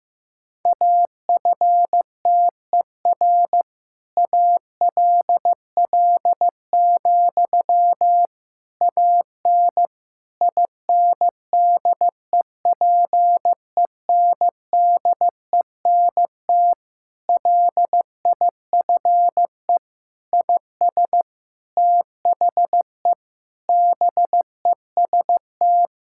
Morse code, heard as a keyed tone, about 16 wpm, 700 Hz. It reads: AFTER ALL, AN INDEPENDENT LIFE IS THE BEST
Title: Sentences from Top 1000 Words - Review Part 6 - 15wpm